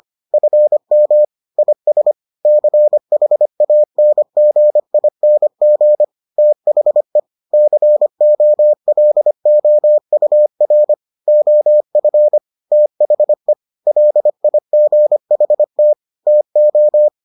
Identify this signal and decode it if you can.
Morse code, heard as a keyed tone, about 25 wpm, 600 Hz. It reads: FM IS CHANGING THE COLOUR OF THE LIGHT TO